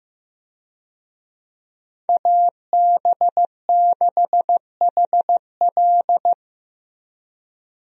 Morse code, heard as a keyed tone, about 15 wpm, 700 Hz. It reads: AB6HL
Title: USA Call Signs - 5 Characters - Repeated 3x - 15wpm